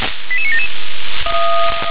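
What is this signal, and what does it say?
Five-tone selective call (ZVEI), heard as quick runs of five tones, tones of about 70 ms. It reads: 88977
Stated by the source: ZVEI five-tone selective call received on SDR.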